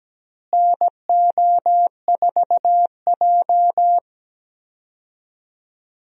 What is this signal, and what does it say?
Morse code, heard as a keyed tone, about 17 wpm, 700 Hz. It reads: NO4J
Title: USA Call Signs - 4 Characters - Repeated 3x - 17wpm